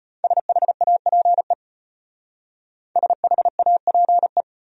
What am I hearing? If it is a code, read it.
Morse code, heard as a keyed tone, about 31 wpm, 700 Hz. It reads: SHAPE SHAPE
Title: Speed-Racing Top 500 Words 25wpm